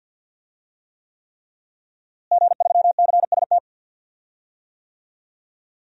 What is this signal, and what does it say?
Morse code, heard as a keyed tone, about 49 wpm, 700 Hz. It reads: G3CST
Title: DXCC-100 Call Signs - 5 Characters - Repeated 3x - 50wpm